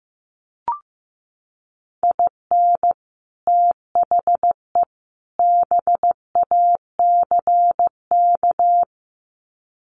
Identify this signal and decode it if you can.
Morse code, heard as a keyed tone, about 15 wpm, 700 Hz. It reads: IN THE BACK
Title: Sets of 3 Words 15wpm